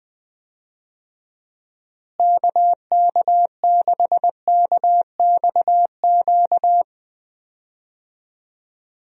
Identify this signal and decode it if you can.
Morse code, heard as a keyed tone, about 20 wpm, 700 Hz. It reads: KK6KXQ
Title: USA Call Signs - 6 Characters - Repeated 2x - 20wpm